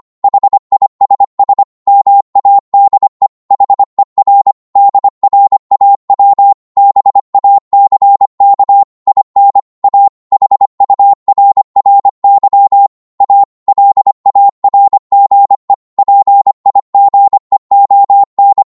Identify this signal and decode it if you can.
Morse code, heard as a keyed tone, about 25 wpm, 800 Hz. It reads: HISS MADE HER DRAW BACK IN A HURRY A LARGE PIGEON
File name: Morse / psg-25wpm